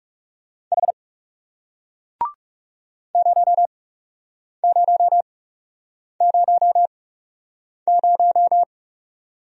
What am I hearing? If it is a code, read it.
Morse code, heard as a keyed tone, about 37 wpm, spaced as for 8 wpm, 700 Hz. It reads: H 0000
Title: Speed-Racing Single Letter-Number 30wpm